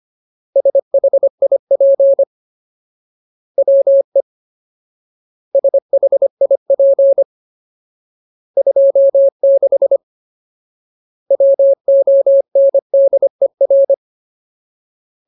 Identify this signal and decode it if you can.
Morse code, heard as a keyed tone, about 25 wpm, 550 Hz. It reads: SHIP WE SHIP 26 WONDER
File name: Ultimate Word List / 25WPM Ultimate